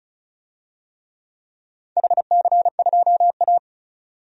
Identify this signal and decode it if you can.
Morse code, heard as a keyed tone, about 34 wpm, 700 Hz. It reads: HC2A
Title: DXCC-100 Call Signs - 4 Characters - 35wpm